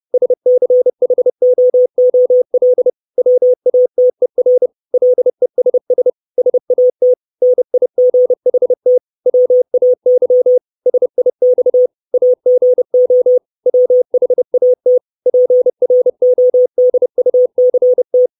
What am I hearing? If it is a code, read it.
Morse code, heard as a keyed tone, about 30 wpm, 500 Hz. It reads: SCHOOL WATER LESS SAT NIGHT WAY SIX AGO WHAT PRODUCT
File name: 632 Words 30WPM